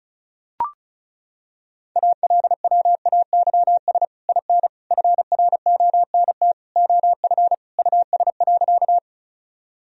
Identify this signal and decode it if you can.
Morse code, heard as a keyed tone, about 36 wpm, 700 Hz. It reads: ALWAYS IN FRONT OF US.
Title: Sentences from Top 1000 Words - Review Part 2 - 35wpm